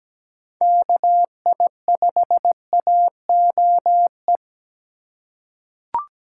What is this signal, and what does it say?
Morse code, heard as a keyed tone, about 17 wpm, 700 Hz. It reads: KI5AOE
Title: USA Call Signs - 6 Characters - 2x Character Spacing - 17wpm